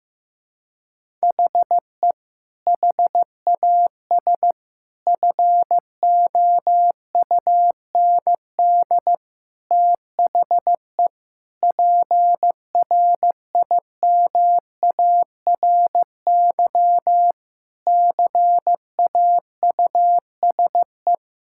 Morse code, heard as a keyed tone, about 15 wpm, 700 Hz. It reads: HE HAS FOUND THE PRIMARY CAUSE
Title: Sentences from Top 1000 Words - Review Part 5 - 15wpm